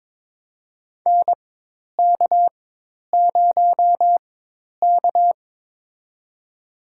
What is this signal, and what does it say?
Morse code, heard as a keyed tone, about 22 wpm, spaced as for 10 wpm, 700 Hz. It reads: NK0K
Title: USA Call Signs - 4 Characters - 4x Character Spacing - 22wpm